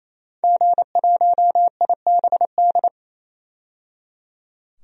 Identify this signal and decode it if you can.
Morse code, heard as a keyed tone, about 28 wpm, 700 Hz. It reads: G1IBD
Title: DXCC-100 Call Signs - 5 Characters - 28wpm